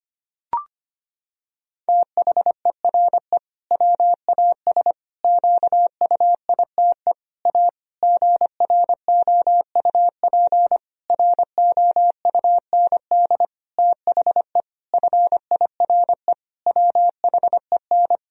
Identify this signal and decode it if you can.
Morse code, heard as a keyed tone, about 25 wpm, 700 Hz. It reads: THERE WAS QUITE A GROUP ROUND THE FIRE WHEN
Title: Sentences from Top 1000 Words - Review Part 1 - 25wpm